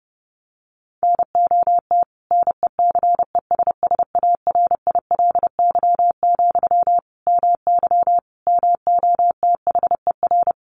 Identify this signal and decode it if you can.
Morse code, heard as a keyed tone, about 30 wpm, 700 Hz. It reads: NOT NECESSARILY, MY MOTHER
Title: Sentences from Top 1400 Words 30wpm